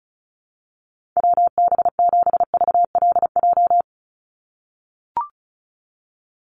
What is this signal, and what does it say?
Morse code, heard as a keyed tone, about 34 wpm, 700 Hz. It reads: WB7VLJ
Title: USA Call Signs 35wpm